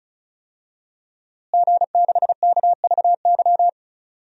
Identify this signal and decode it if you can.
Morse code, heard as a keyed tone, about 36 wpm, 700 Hz. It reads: G6KVY
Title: DXCC-100 Call Signs - 5 Characters - Repeated 2x - 35wpm